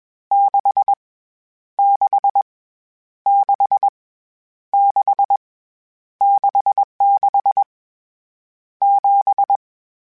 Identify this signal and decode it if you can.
Morse code, heard as a keyed tone, about 21 wpm, 800 Hz. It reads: 6 6 6 6 66 7
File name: Misc / Misc-mp3s / Numb @20